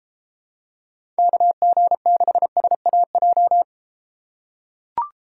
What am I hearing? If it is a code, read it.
Morse code, heard as a keyed tone, about 33 wpm, 700 Hz. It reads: KG6SAJ